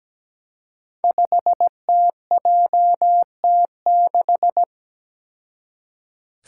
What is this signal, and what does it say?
Morse code, heard as a keyed tone, about 17 wpm, 700 Hz. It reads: 5TJT6
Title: Five Challenging Letter-Number Combinations 17wpm